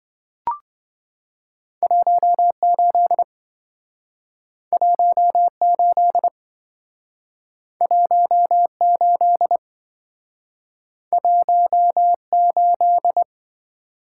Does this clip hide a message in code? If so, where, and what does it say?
Yes: Morse code from 1.8 to 13.2 s, heard as a keyed tone, about 26 wpm, 700 Hz. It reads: 18 18 18 18